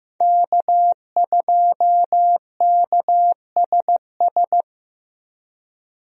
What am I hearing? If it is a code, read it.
Morse code, heard as a keyed tone, about 16 wpm, 700 Hz. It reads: K2KSS
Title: USA Call Signs - 5 Characters - Repeated 2x - 15wpm